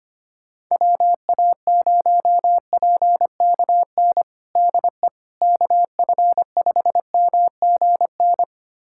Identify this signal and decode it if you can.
Morse code, heard as a keyed tone, about 25 wpm, 700 Hz. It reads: WA0PKN DE KF5MGN
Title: USA Callsign DE Callsign - 6 Characters - 25wpm